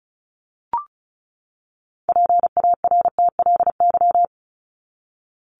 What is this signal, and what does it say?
Morse code, heard as a keyed tone, about 34 wpm, 700 Hz. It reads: PARTLY
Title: LY Suffix - Top 100 Words with Base Words 35wpm